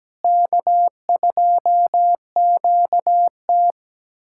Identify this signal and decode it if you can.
Morse code, heard as a keyed tone, about 17 wpm, 700 Hz. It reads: K2QT
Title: USA Callsign DE Callsign - 4 Characters - 17wpm